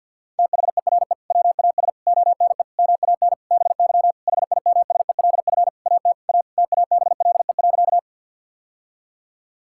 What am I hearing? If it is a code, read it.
Morse code, heard as a keyed tone, about 50 wpm, 700 Hz. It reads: THERE WAS ONE MAN BY HIMSELF AT A TABLE.
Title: Sentences from Top 1600 Words 50wpm